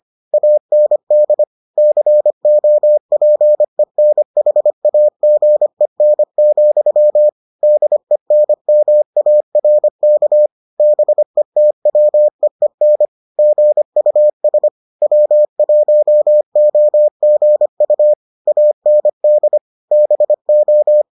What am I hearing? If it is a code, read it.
Morse code, heard as a keyed tone, about 25 wpm, 600 Hz. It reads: AND COPENHAGEN, DENMARK BETWEEN GUS W1OGU AND BO